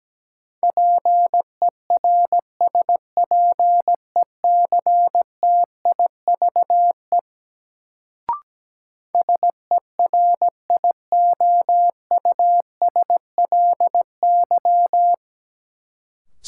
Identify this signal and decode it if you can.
Morse code, heard as a keyed tone, about 17 wpm, 700 Hz. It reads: PERSPECTIVE SERIOUSLY
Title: Sentences from Top 1200 Words - New Word Drill Alt - 17wpm